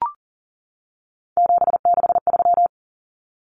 Morse code, heard as a keyed tone, about 40 wpm, 700 Hz. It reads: 763
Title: Numbers - 3-Digit 40wpm